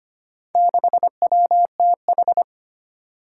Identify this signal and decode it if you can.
Morse code, heard as a keyed tone, about 25 wpm, 700 Hz. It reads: 6WTH
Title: Four Challenging Letter-Number Combinations 25wpm